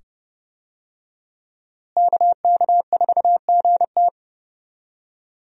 Morse code, heard as a keyed tone, about 30 wpm, 700 Hz. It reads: KK4GT
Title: USA Call Signs - 5 Characters - Repeated 2x - 30wpm